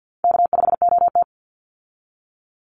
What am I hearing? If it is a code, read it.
Morse code, heard as a keyed tone, about 50 wpm, 700 Hz. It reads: K5OT
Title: Easy Call Signs Repeated 2x - 50wpm